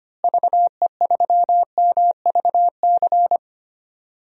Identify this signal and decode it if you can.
Morse code, heard as a keyed tone, about 25 wpm, 700 Hz. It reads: VE3MVC
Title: DXCC-100 Call Signs - 6 Characters - Repeated 3x - 25wpm